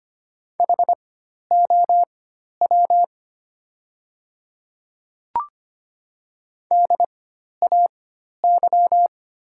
Morse code, heard as a keyed tone, about 25 wpm, spaced as for 12 wpm, 700 Hz. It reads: HOW DAY
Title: 4x Character Spacing - Top 100 Words - 25wpm